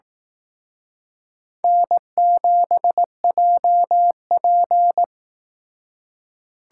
Morse code, heard as a keyed tone, about 19 wpm, 700 Hz. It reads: N7JP
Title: Speed Racing - USA 1x2 Call Signs 15wpm